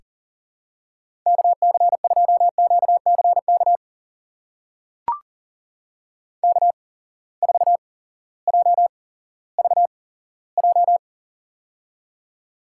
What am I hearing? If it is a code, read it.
Morse code, heard as a keyed tone, about 38 wpm, spaced as for 18 wpm, 700 Hz. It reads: KC2QCK K4JVJ
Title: USA Call Signs - 8x Character Spacing - 40wpm